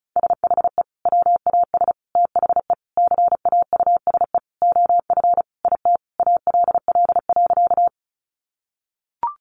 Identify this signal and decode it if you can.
Morse code, heard as a keyed tone, about 34 wpm, 700 Hz. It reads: SHE WAS THE CAUSE OF IT ALL.
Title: Sentences from Top 200 Words 35wpm